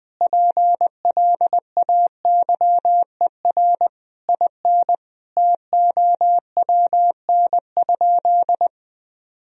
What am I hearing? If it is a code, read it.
Morse code, heard as a keyed tone, about 20 wpm, 700 Hz. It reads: PLAYER IN TOWN?
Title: Sentences from Top 1000 Words - Review Part 4 - 20wpm